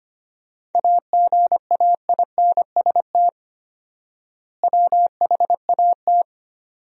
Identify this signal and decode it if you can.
Morse code, heard as a keyed tone, about 25 wpm, 700 Hz. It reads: AGAINST WHAT
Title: 4x Word-Spacing - Sentences from Top 400 Words - 25wpm